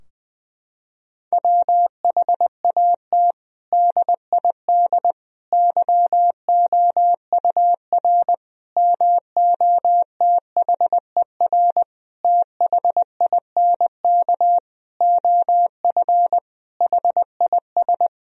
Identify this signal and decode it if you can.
Morse code, heard as a keyed tone, about 20 wpm, 700 Hz. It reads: WHAT DID YOUR MOTHER THINK OF HIS
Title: Sentences from Top 1000 Words - Review Part 4 - 20wpm